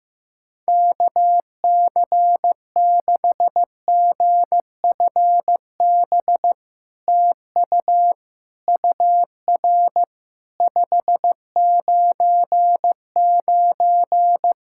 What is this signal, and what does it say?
Morse code, heard as a keyed tone, about 15 wpm, 700 Hz. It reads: KC6GFB TU UR 599
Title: SOTA QSO Practice 15wpm